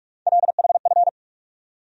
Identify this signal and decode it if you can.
Morse code, heard as a keyed tone, about 48 wpm, 700 Hz. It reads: LHF
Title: USA Call Sign Postfixes - 3 Letters 45wpm